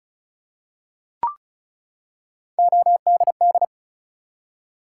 Morse code, heard as a keyed tone, about 36 wpm, 700 Hz. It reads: ODD
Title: Three Letter Words - 35wpm